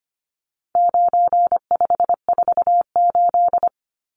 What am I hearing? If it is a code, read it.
Morse code, heard as a keyed tone, about 25 wpm, 700 Hz. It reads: 9548
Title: Numbers - 4-Digit 25wpm